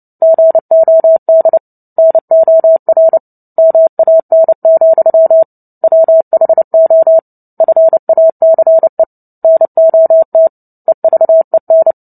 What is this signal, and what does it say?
Morse code, heard as a keyed tone, about 30 wpm, 650 Hz. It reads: GOD NOR MAN, WHO FACE NOT EVEN
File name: Book Two / 03-Chapter 3 - 29 WPM